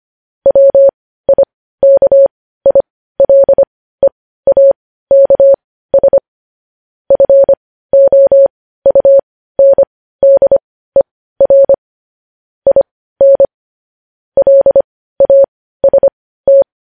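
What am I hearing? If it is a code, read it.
Morse code, heard as a keyed tone, about 25 wpm, spaced as for 15 wpm, 550 Hz. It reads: WIKILEAKS FOUNDER IN LAST
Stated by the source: Morse code news headlines.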